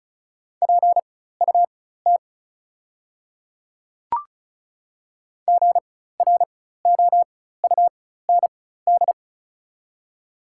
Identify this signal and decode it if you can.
Morse code, heard as a keyed tone, about 36 wpm, spaced as for 16 wpm, 690 Hz. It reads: PUT GROUND
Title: 4x Character Spacing - Top 500 Words - 35wpm